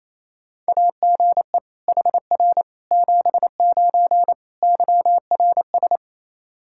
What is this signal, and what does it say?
Morse code, heard as a keyed tone, about 27 wpm, 700 Hz. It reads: AGE HR 79 YRS
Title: K1USN SST QSO Practice 28wpm